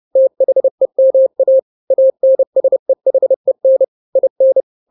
Morse code, heard as a keyed tone, about 30 wpm, 530 Hz. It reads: THEMA ANSEHEN IN
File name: daily / 29wpm